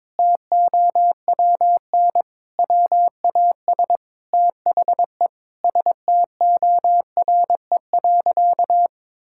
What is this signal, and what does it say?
Morse code, heard as a keyed tone, about 22 wpm, 700 Hz. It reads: TOWN WAS THE STORE.
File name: practice / 22wpm / Sentences with 4-Word Phrases - 22wpm